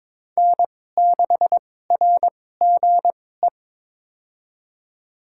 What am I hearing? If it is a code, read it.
Morse code, heard as a keyed tone, about 22 wpm, spaced as for 16 wpm, 700 Hz. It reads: N6RGE
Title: USA Call Signs - 5 Characters - 2x Character Spacing - 22wpm